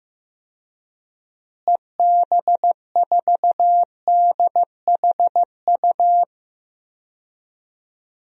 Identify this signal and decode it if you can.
Morse code, heard as a keyed tone, about 16 wpm, 700 Hz. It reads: EB4DHU
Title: DXCC-100 Call Signs - 6 Characters - Repeated 2x - 15wpm